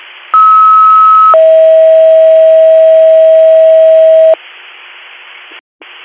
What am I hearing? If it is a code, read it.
Two-tone page, heard as a one-second tone, then a three-second tone, about 1290 then 640 Hz. It is heard from 0.3 to 4.3 s.